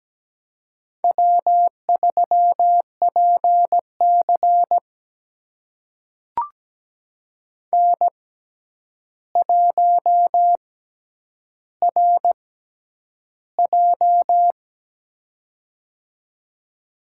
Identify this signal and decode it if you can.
Morse code, heard as a keyed tone, about 17 wpm, spaced as for 9 wpm, 700 Hz. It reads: W3PC N1RJ
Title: USA Call Signs - 4 Characters - 6x Character Spacing - 17wpm